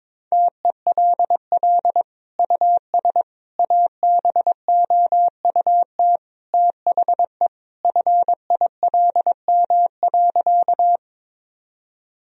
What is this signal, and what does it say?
Morse code, heard as a keyed tone, about 22 wpm, 700 Hz. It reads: TELL US ABOUT THE FILM.
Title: Sentences from Top 1000 Words - Review Part 3 - 22wpm